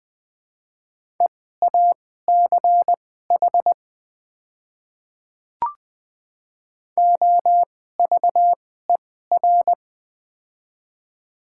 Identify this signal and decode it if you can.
Morse code, heard as a keyed tone, about 20 wpm, spaced as for 15 wpm, 700 Hz. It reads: EACH OVER
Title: 2x Character Spacing - Top 100 Words - 20wpm